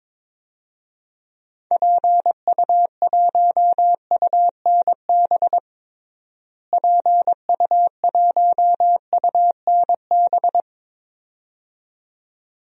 Morse code, heard as a keyed tone, about 22 wpm, 700 Hz. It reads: PU1UNB PU1UNB
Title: DXCC-100 Call Signs - 6 Characters - Repeated 2x - 22wpm